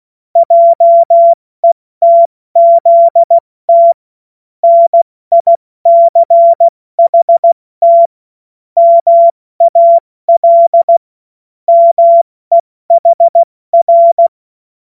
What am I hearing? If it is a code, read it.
Morse code, heard as a keyed tone, about 16 wpm, 680 Hz. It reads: JETZT NICHT MAL MEHR
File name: Morsekurs / Witze / Witz 27. Januar 2026 70 F80